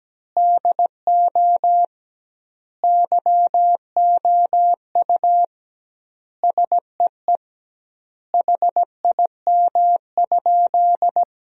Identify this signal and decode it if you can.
Morse code, heard as a keyed tone, about 17 wpm, 700 Hz. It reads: DO YOU SEE HIM?
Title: 2x Word-Spacing - Sentences from Top 100 Words - 17wpm